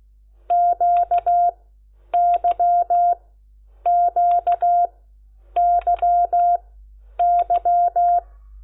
Morse code, heard as a keyed tone, about 16 wpm, spaced as for 9 wpm, 700 Hz. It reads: QYQYY
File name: Lecciones